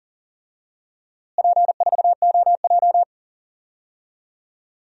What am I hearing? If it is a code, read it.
Morse code, heard as a keyed tone, about 40 wpm, 700 Hz. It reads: P4OJ
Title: DXCC-100 Call Signs - 4 Characters - Repeated 3x - 40wpm